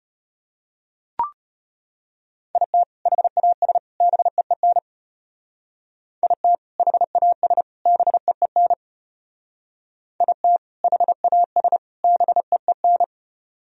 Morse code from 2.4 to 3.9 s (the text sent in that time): IT HAS